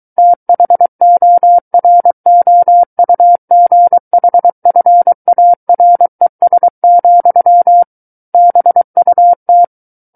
Morse code, heard as a keyed tone, about 23 wpm, 700 Hz. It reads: THOROUGHFARES, BUT
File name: Book One / 14-Chapter 14 - 23 WPM